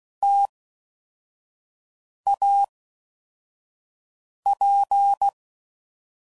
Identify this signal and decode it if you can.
Morse code, heard as a keyed tone, about 16 wpm, 800 Hz. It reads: T A P
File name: Mixed / Variable Sp / T AOEIs